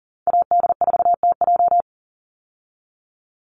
Morse code, heard as a keyed tone, about 40 wpm, 700 Hz. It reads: AD4TJ
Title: Call Signs Repeated 3x - 40wpm